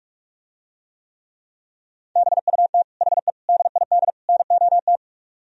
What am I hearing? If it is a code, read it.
Morse code, heard as a keyed tone, about 44 wpm, 700 Hz. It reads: BUT HE DID NOT